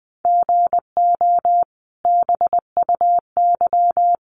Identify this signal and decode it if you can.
Morse code, heard as a keyed tone, about 20 wpm, 700 Hz. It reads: GO BUY
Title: Sentences from Top 1500 Words 20wpm